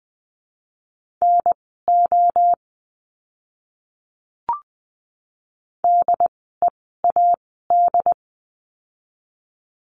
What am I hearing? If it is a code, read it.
Morse code, heard as a keyed tone, about 20 wpm, spaced as for 14 wpm, 700 Hz. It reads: NO DEAD
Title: 075 Lesson 13f - Letter L - Words 20wpm